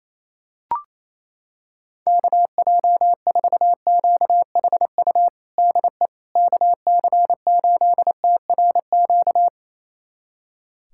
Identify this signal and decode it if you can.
Morse code, heard as a keyed tone, about 28 wpm, 700 Hz. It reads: KJ4QHU DE KC8TRQ
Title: USA Callsign DE Callsign 28wpm